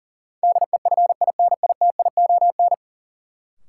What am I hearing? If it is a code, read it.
Morse code, heard as a keyed tone, about 40 wpm, 700 Hz. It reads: DEFINITION
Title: Sentences from Top 1000 Words - Review Part 4 - New Word Drill - 40wpm